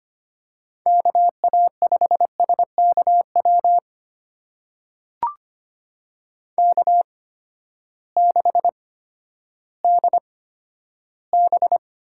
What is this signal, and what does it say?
Morse code, heard as a keyed tone, about 25 wpm, spaced as for 12 wpm, 700 Hz. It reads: KA5SKW K6DB
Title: USA Call Signs - 8x Character Spacing - 25wpm